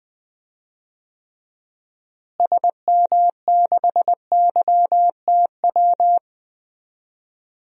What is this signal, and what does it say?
Morse code, heard as a keyed tone, about 21 wpm, 700 Hz. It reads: SM6YTW
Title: DXCC-100 Call Signs - 6 Characters - Repeated 2x - 20wpm